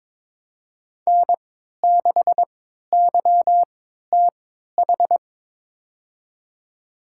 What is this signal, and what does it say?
Morse code, heard as a keyed tone, about 22 wpm, spaced as for 13 wpm, 700 Hz. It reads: N6YTH
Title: USA Call Signs - 3x Character Spacing - 22wpm